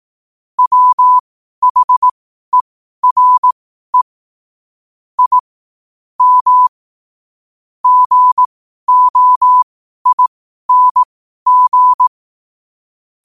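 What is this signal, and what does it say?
Morse code, heard as a keyed tone, about 18 wpm, spaced as for 12 wpm, 1000 Hz. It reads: WHERE I M GOING
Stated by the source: Morse code at 12 wpm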